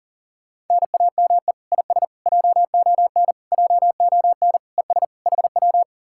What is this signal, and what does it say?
Morse code, heard as a keyed tone, about 52 wpm, 700 Hz. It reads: NAME IS JON JON ES HW